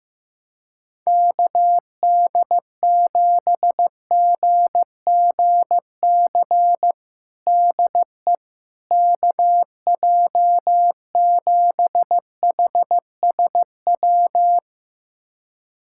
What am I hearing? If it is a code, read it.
Morse code, heard as a keyed tone, about 15 wpm, 700 Hz. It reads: KD7GGC DE KJ7HSW
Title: USA Callsign DE Callsign - 6 Characters - 15wpm